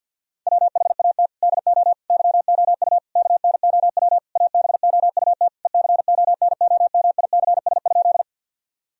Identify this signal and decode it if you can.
Morse code, heard as a keyed tone, about 51 wpm, 700 Hz. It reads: WHAT DO YOU KNOW ABOUT ECONOMICS?
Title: Sentences from Top 1000 Words - Review Part 3 - 50wpm